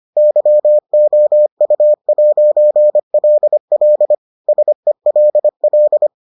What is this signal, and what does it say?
Morse code, heard as a keyed tone, about 25 wpm, 600 Hz. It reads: YOU'LL SELL